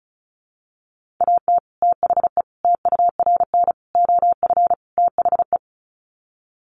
Morse code, heard as a keyed tone, about 34 wpm, 700 Hz. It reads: AT THE TURN OF THE
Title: Sets of 5 Words 35wpm